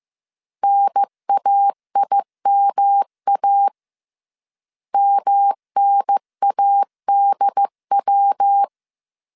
Morse code, heard as a keyed tone, about 15 wpm, 800 Hz. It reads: NAIMA MNADW
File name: CW / DET / 1 3 1500C S C 14wpm 2458